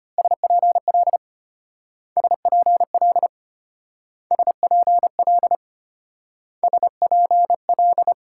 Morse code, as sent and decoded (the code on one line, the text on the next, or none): ... .--. .-.. / ... .--. .-.. / ... .--. .-.. / ... .--. .-..
SPL SPL SPL SPL